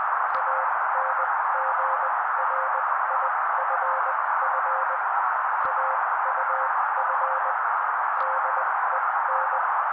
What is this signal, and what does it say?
Morse code, heard as a keyed tone, about 20 wpm, spaced as for 16 wpm, 530 Hz. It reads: ANGRIFF AUF DEN
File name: daily / 16wpm noise